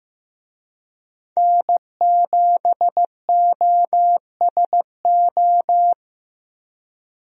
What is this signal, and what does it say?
Morse code, heard as a keyed tone, about 15 wpm, 700 Hz. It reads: N7OSO
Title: USA Call Signs - 5 Characters - 15wpm